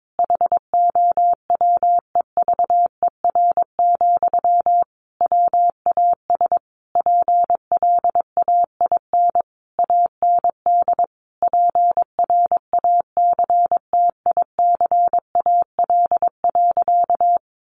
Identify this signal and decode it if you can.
Morse code, heard as a keyed tone, about 22 wpm, 700 Hz. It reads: HOWEVER, WAS PLAIN AND PRACTICAL.
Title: Sentences from Top 1400 Words 22wpm